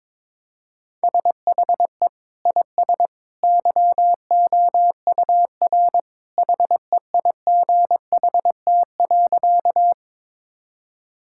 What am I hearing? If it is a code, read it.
Morse code, heard as a keyed tone, about 22 wpm, 700 Hz. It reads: SHE IS YOUR HEIGHT.
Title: Sentences from Top 1600 Words 22wpm